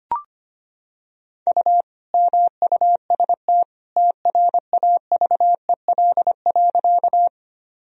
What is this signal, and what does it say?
Morse code, heard as a keyed tone, about 25 wpm, 700 Hz. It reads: U MUST TRAVEL.
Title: Sentences from Top 400 Words and Abbreviations 25wpm